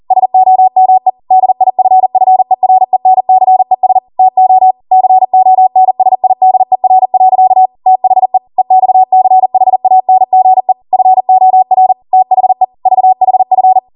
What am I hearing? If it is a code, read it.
Morse code, heard as a keyed tone, about 40 wpm, 750 Hz. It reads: SOME DIFFERENCES TO CONSIDER. THE EXCHANGE FOR THE VHF